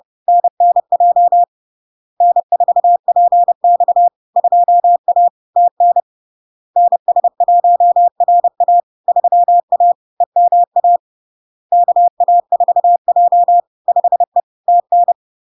Morse code, heard as a keyed tone, about 30 wpm, 700 Hz. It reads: NNJ N4PX 2A TN NS1RA 3A EMA KA4J 5E TN
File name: LARK-FD-2022-Log-in-CW-30wpm-700-Hz